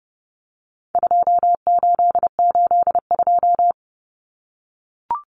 Morse code, heard as a keyed tone, about 30 wpm, 700 Hz. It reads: 2882
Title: Common Cut Numbers - 4-Digit 30wpm